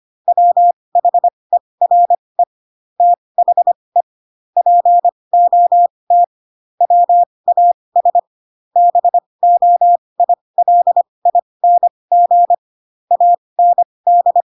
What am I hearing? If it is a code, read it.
Morse code, heard as a keyed tone, about 25 wpm, spaced as for 20 wpm, 700 Hz. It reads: WHERE THE POT WAS BOILING AND